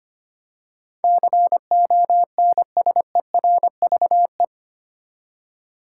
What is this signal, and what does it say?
Morse code, heard as a keyed tone, about 25 wpm, 700 Hz. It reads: CONSERVE
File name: Eight Letter Words - 25wpm